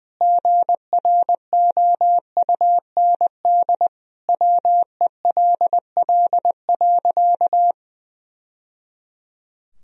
Morse code, heard as a keyed tone, about 20 wpm, 700 Hz. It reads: GROUND WELL.
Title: Sentences from Top 1000 Words - Review Part 2 - 20wpm